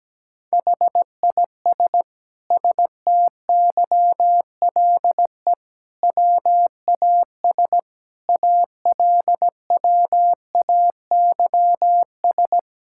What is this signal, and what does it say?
Morse code, heard as a keyed tone, about 17 wpm, 700 Hz. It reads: HIS STYLE WAS ALWAYS
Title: Sentences from Top 1000 Words - Review Part 5 - 17wpm